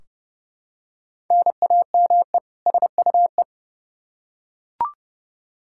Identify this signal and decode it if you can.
Morse code, heard as a keyed tone, about 30 wpm, 700 Hz. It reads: NAME SUE
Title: Practice QSO - Exchange 1 - Beginning - 30wpm